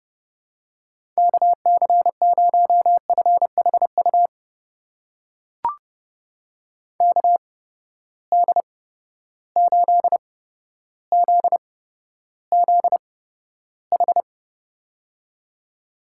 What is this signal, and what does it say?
Morse code, heard as a keyed tone, about 30 wpm, spaced as for 13 wpm, 700 Hz. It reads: KC0FHU KD8ZZH
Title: USA Call Signs - 6 Characters - 8x Character Spacing - 30wpm